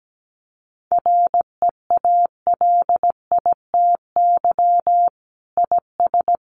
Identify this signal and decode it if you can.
Morse code, heard as a keyed tone, about 17 wpm, 700 Hz. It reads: REALITY IS
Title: Sentences from Top 1200 Words 17wpm